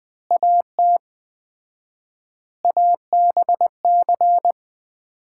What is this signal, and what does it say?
Morse code, heard as a keyed tone, about 21 wpm, 700 Hz. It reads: AT ABC
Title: 4x Word-Spacing - Sets of 5 Words - 20wpm